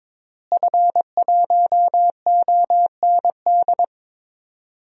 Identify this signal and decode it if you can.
Morse code, heard as a keyed tone, about 22 wpm, 700 Hz. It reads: F1OND
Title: DXCC-100 Call Signs - 5 Characters - Repeated 2x - 22wpm